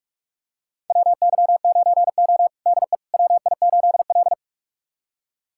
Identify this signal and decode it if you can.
Morse code, heard as a keyed tone, about 46 wpm, 700 Hz. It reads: WY9O DE WI8L
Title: USA Callsign DE Callsign - 4 Characters - 45wpm